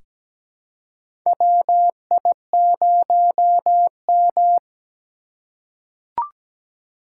Morse code, heard as a keyed tone, about 17 wpm, 700 Hz. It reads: WI0M
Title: USA Call Signs - 4 Characters - 4x Character Spacing - 17wpm